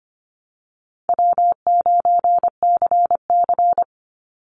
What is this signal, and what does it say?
Morse code, heard as a keyed tone, about 25 wpm, 700 Hz. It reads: W9CC
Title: Easy Call Signs - 25wpm